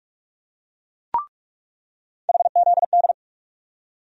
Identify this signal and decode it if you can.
Morse code, heard as a keyed tone, about 46 wpm, 700 Hz. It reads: HZD
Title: USA Call Sign Postfixes - 3 Letters 45wpm